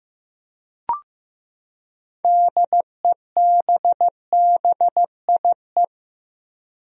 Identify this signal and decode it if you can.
Morse code, heard as a keyed tone, about 15 wpm, 700 Hz. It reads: DEBBIE
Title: USA Common Names - 15wpm